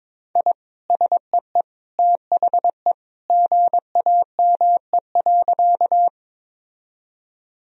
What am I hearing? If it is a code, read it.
Morse code, heard as a keyed tone, about 22 wpm, 700 Hz. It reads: I SEE THE GAME.
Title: Sentences from Top 1000 Words - Review Part 3 - 22wpm